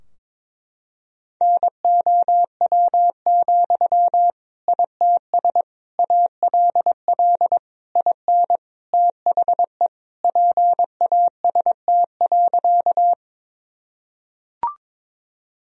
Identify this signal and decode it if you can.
Morse code, heard as a keyed tone, about 22 wpm, 700 Hz. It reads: NOW, ITS ALL IN THE PAST.
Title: Sentences from Top 1000 Words - Review Part 3 - 22wpm